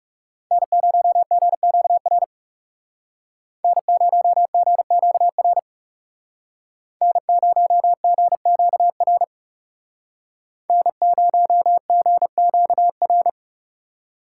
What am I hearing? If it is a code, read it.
Morse code, heard as a keyed tone, about 37 wpm, 700 Hz. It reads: N0GQR N0GQR N0GQR N0GQR
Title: Speed Racing - USA 1x3 Call Signs 30wpm